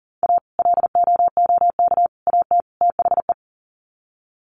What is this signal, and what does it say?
Morse code, heard as a keyed tone, about 40 wpm, 700 Hz. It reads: A LOOK AT THE
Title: Sets of 4 Words 40wpm